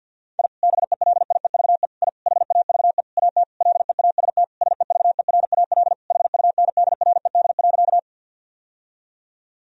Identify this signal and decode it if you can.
Morse code, heard as a keyed tone, about 51 wpm, 700 Hz. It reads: I BELIEVE I HAVE AT LEAST SEVERAL HUNDRED.
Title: Sentences from Top 1000 Words - Review Part 2 - 50wpm